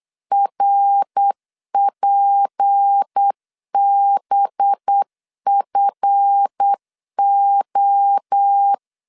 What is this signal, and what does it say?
Morse code, heard as a keyed tone, about 8 wpm, 800 Hz. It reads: RPBFO
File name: LEZIONI / 8 YJ45 / 8 YJ45 8wpm 3214